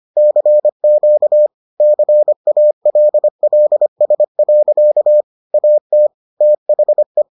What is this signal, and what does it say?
Morse code, heard as a keyed tone, about 25 wpm, 600 Hz. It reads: CQ CALLS. AT THE